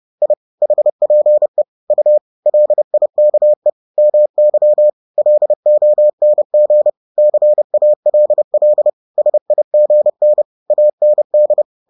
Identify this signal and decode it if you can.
Morse code, heard as a keyed tone, about 30 wpm, 600 Hz. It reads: I HPE U LIKE MY LONG CALL SIGN AND